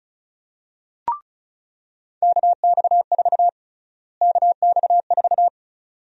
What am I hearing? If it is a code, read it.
Morse code, heard as a keyed tone, about 36 wpm, 700 Hz. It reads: KX4 KX4
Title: USA Call Sign Prefixes - Repeated 2x - 35wpm